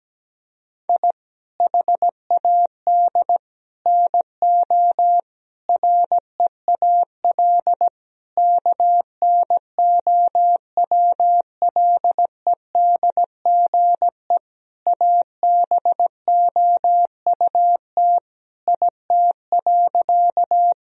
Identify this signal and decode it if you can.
Morse code, heard as a keyed tone, about 17 wpm, 700 Hz. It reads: I HAD NO REAL KNOWLEDGE ABOUT IT.
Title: Sentences from Top 1000 Words - Review Part 3 - 17wpm